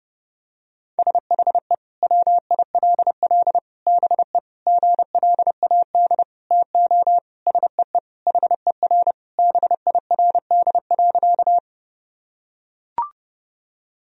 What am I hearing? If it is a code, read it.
Morse code, heard as a keyed tone, about 30 wpm, 700 Hz. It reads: SHE WILL BE GLAD TO SEE HER BIRD.
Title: Sentences from Top 1000 Words - Review Part 2 - 30wpm